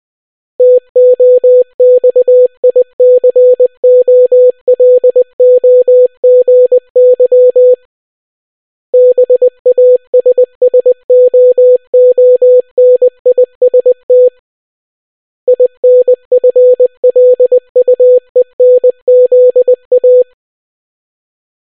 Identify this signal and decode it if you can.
Morse code, heard as a keyed tone, about 20 wpm, 500 Hz. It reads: TOXICOLOGY BASSOONIST INFLUENZA